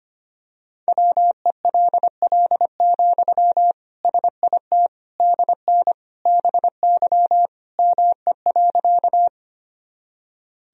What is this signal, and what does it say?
Morse code, heard as a keyed tone, about 25 wpm, 700 Hz. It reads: WELL, SIT DN BY ME.
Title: Sentences from Top 500 Words and Abbreviations 25wpm